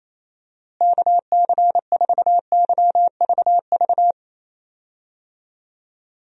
Morse code, heard as a keyed tone, about 28 wpm, 700 Hz. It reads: KC4YVV
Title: USA Call Signs - 6 Characters - Repeated 3x - 28wpm